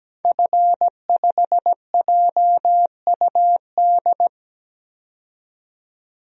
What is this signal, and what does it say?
Morse code, heard as a keyed tone, about 17 wpm, 700 Hz. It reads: F5JUD
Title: DXCC-100 Call Signs - 5 Characters - 17wpm